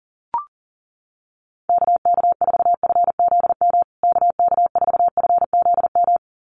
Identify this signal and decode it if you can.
Morse code, heard as a keyed tone, about 40 wpm, 700 Hz. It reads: KK4FZM KK4FZM
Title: USA Call Signs Repeated 2x 40wpm